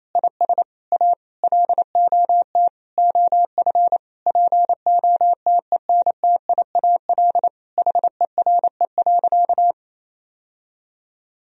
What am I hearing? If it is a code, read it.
Morse code, heard as a keyed tone, about 28 wpm, 700 Hz. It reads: IS A LOT OF POTENTIAL HERE.
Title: Sentences from Top 1000 Words - Review Part 4 - 28wpm